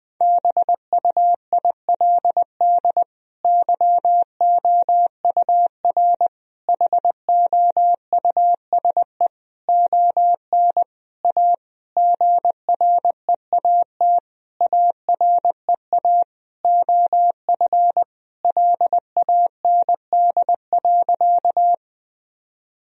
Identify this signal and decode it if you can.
Morse code, heard as a keyed tone, about 20 wpm, 700 Hz. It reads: BUILD YOUR HOUSE ON A GREAT AREA OF LAND.
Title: Sentences from Top 1000 Words - Review Part 2 - 20wpm